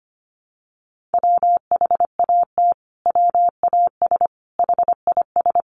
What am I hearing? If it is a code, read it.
Morse code, heard as a keyed tone, about 25 wpm, 700 Hz. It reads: WHAT WAS HIS
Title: Sentences from Top 1300 Words 25wpm